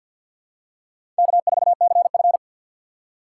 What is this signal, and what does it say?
Morse code, heard as a keyed tone, about 51 wpm, 690 Hz. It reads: K4CF
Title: USA Call Signs - 6x Character Spacing - 50wpm